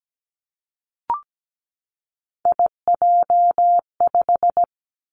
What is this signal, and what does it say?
Morse code, heard as a keyed tone, about 17 wpm, 700 Hz. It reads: IJ5
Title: Three Challenging Letter-Number Combinations 17wpm